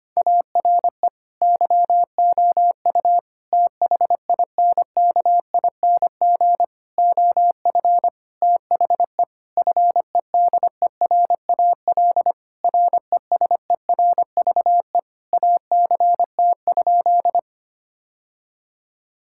Morse code, heard as a keyed tone, about 25 wpm, 700 Hz. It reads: ARE YOU THINKING OF THE FEDERAL RESERVE ACT?
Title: Sentences from Top 1000 Words - Review Part 6 - 25wpm